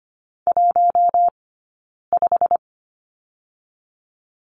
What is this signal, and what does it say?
Morse code, heard as a keyed tone, about 25 wpm, 700 Hz. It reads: 1 5
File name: Lesson 17d - Letter u - Two Characters 25wpm